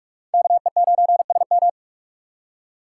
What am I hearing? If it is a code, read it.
Morse code, heard as a keyed tone, about 38 wpm, 700 Hz. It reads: KE9SM